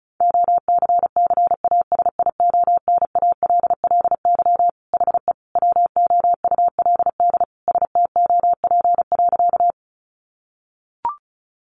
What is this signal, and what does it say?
Morse code, heard as a keyed tone, about 34 wpm, 700 Hz. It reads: OCCASIONALLY HE WOULD STOP.
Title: Sentences from Top 1300 Words 35wpm